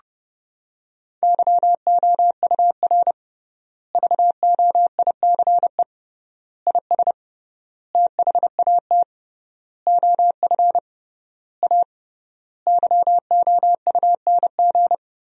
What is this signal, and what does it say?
Morse code, heard as a keyed tone, about 30 wpm, 700 Hz. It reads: YOUR VOICE IS THAT OF A YOUNG
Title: 3x Word-Spacing - Sentences from Top 500 Words - 30wpm